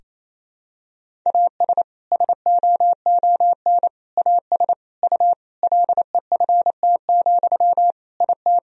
Morse code, heard as a keyed tone, about 27 wpm, 700 Hz. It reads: AS SOON AS U LEFT, IT
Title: Sentences from Top 400 Words and Abbreviations 28wpm